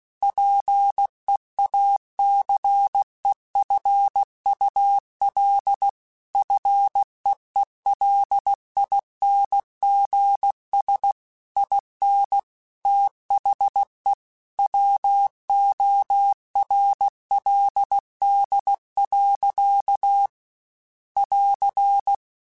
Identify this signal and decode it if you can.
Morse code, heard as a keyed tone, about 16 wpm, 800 Hz. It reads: PEACEFUL FEELINGS IN THE WORLD. <AR>